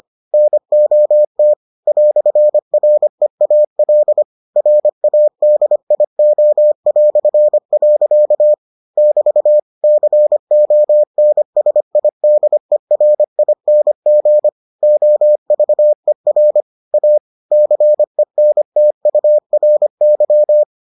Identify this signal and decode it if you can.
Morse code, heard as a keyed tone, about 25 wpm, 600 Hz. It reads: NOT "REAL RADIO". <BT> CONSIDERING OVER A CENTURY